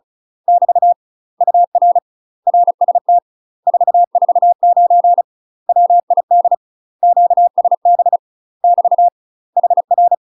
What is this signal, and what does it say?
Morse code, heard as a keyed tone, about 35 wpm, 700 Hz. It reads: <BT> UR RST 449 WID QSB <BT> HR
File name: UK MASTER QSOs TRAINING / 35WPM / UK MASTER QSOs 0029